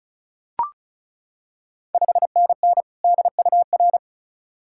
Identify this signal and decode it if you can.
Morse code, heard as a keyed tone, about 34 wpm, 690 Hz. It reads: 5NN DUR
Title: Ontario QSO Party Exchange 35wpm